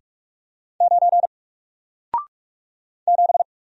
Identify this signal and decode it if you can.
Morse code, heard as a keyed tone, about 45 wpm, 700 Hz. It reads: 9 7
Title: Single Letter-Number Alt 45wpm